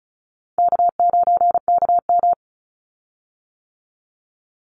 Morse code, heard as a keyed tone, about 34 wpm, 700 Hz. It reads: K9KM
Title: Call Signs - 35wpm